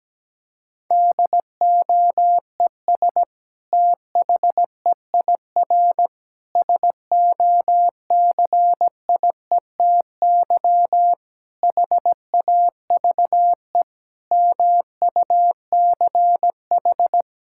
Morse code, heard as a keyed tone, about 17 wpm, 700 Hz. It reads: DOES THEIR SOCIETY HAVE MUCH
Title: Sentences from Top 1000 Words - Review Part 6 - 17wpm